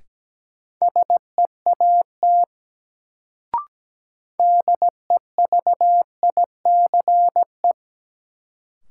Morse code, heard as a keyed tone, about 17 wpm, 700 Hz. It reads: SEAT DEVICE
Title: Sentences from Top 1500 Words - New Word Drill - 17wpm